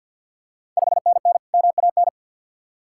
Morse code, heard as a keyed tone, about 51 wpm, 700 Hz. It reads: 5NN MAN
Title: Ontario QSO Party Exchange 50wpm